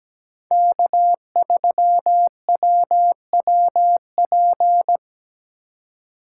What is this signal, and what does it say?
Morse code, heard as a keyed tone, about 17 wpm, 700 Hz. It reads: K3WWP
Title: K1USN SST QSO Practice - 17wpm